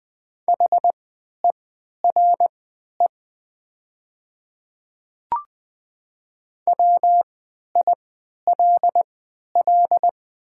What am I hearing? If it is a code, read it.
Morse code, heard as a keyed tone, about 20 wpm, spaced as for 11 wpm, 690 Hz. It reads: HERE WILL
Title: 3x Character Spacing - Top 500 Words - 20wpm